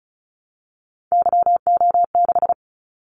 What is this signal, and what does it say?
Morse code, heard as a keyed tone, about 34 wpm, 700 Hz. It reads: YO6
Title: Callsign Prefixes for Top 25 DXCC - Repeated 3x - 35wpm